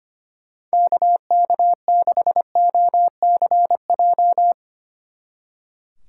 Morse code, heard as a keyed tone, about 31 wpm, 700 Hz. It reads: KK6OCJ
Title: Speed Racing - USA Call Signs 25wpm